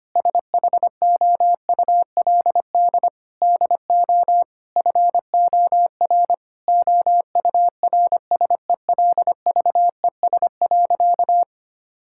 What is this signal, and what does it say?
Morse code, heard as a keyed tone, about 25 wpm, 700 Hz. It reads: SHOULD DO FOR OURSELVES.
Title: Sentences from Top 1600 Words 25wpm